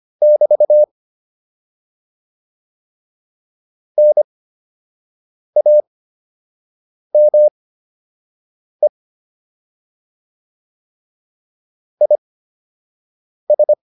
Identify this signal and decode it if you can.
Morse code, heard as a keyed tone, about 25 wpm, spaced as for 6 wpm, 600 Hz. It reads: <BT> NAME IS
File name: QSO 105 06